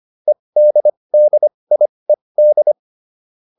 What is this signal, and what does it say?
Morse code, heard as a keyed tone, about 25 wpm, spaced as for 20 wpm, 600 Hz. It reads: EDDIED